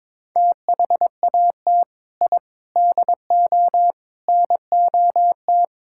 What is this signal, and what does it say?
Morse code, heard as a keyed tone, about 22 wpm, 700 Hz. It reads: THAT I DO NOT
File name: practice / 22wpm / Sentences with 4-Word Phrases - 22wpm